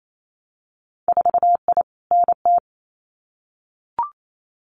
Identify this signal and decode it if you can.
Morse code, heard as a keyed tone, about 28 wpm, 700 Hz. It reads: 4I NT
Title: Winter Field Day Exchanges Encore 28wpm